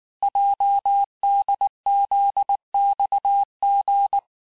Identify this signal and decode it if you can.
Morse code, heard as a keyed tone, about 19 wpm, 800 Hz. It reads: JDZXG
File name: cours-du-lundi files / cours-1589-texte-chiffre-vitesse-18mpm